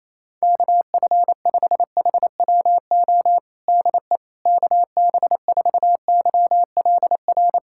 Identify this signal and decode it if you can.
Morse code, heard as a keyed tone, about 28 wpm, 700 Hz. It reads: KF5HWO DE KB4YLR
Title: USA Callsign DE Callsign - 6 Characters - 28wpm